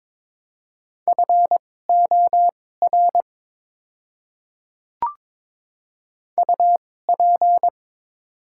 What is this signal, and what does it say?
Morse code, heard as a keyed tone, about 22 wpm, spaced as for 16 wpm, 700 Hz. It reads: FOR UP
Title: 2x Character Spacing - Top 200 Words - 22wpm